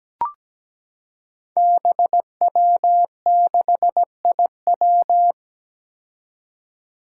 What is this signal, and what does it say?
Morse code, heard as a keyed tone, about 17 wpm, 700 Hz. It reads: BW6IW
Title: Five Challenging Letter-Number Combinations 17wpm